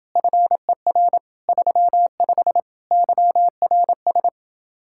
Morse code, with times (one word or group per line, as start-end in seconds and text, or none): 0.2-1.2 FER
1.5-2.6 35
2.9-4.3 YRS